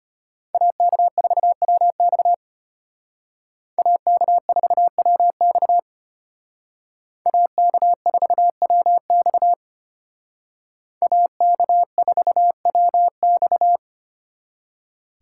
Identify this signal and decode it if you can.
Morse code, heard as a keyed tone, about 30 wpm, 700 Hz. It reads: AK4WX AK4WX AK4WX AK4WX
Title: Speed Racing - USA 2x2 Call Signs 25wpm